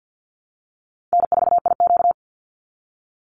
Morse code, heard as a keyed tone, about 50 wpm, 700 Hz. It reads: N4IQ
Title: Call Signs Repeated 2x - 50wpm